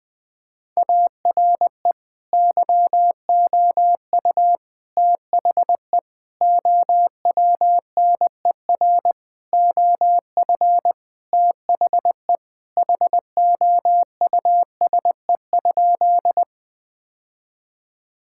Morse code, heard as a keyed tone, about 20 wpm, 700 Hz. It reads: ARE YOU THE OWNER OF THE HOUSE?
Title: Sentences from Top 1600 Words 20wpm